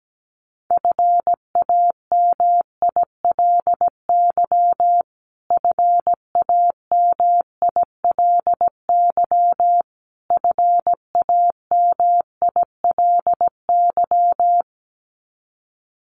Morse code, heard as a keyed tone, about 17 wpm, 700 Hz. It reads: FAMILY FAMILY FAMILY
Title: Top 500 Words Repeated 17wpm